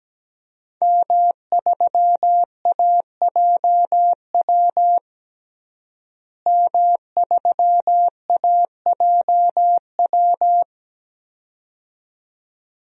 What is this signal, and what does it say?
Morse code, heard as a keyed tone, about 17 wpm, 700 Hz. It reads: M3AJW M3AJW
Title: DXCC-100 Call Signs - 5 Characters - Repeated 2x - 17wpm